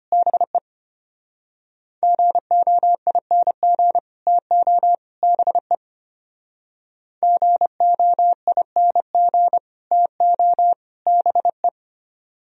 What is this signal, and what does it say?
Morse code, heard as a keyed tone, about 33 wpm, 700 Hz. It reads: BE GOING TO BE GOING TO BE